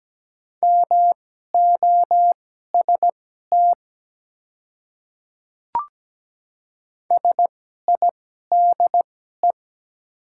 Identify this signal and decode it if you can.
Morse code, heard as a keyed tone, about 17 wpm, spaced as for 12 wpm, 700 Hz. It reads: MOST SIDE
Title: 2x Character Spacing - Top 300 Words - 17wpm